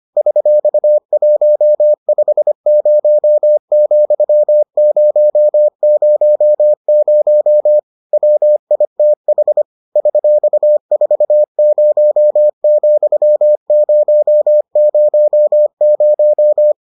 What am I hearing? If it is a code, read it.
Morse code, heard as a keyed tone, about 25 wpm, 600 Hz. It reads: $150,000 WITH $40,000